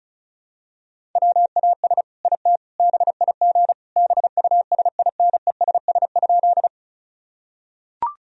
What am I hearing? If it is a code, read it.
Morse code, heard as a keyed tone, about 36 wpm, 700 Hz. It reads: WAS IT BIG BUSINESS?
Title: Sentences from Top 1000 Words - Review Part 1 - 35wpm